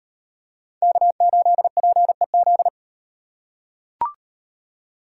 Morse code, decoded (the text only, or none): K8PEZ